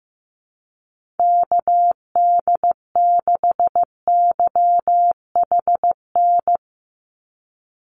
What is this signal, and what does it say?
Morse code, heard as a keyed tone, about 15 wpm, 700 Hz. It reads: KD6YHN
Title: USA Call Signs 15wpm